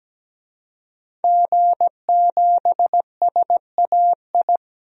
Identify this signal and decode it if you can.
Morse code, heard as a keyed tone, about 17 wpm, 700 Hz. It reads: G7SAI
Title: DXCC-100 Call Signs - 5 Characters - 17wpm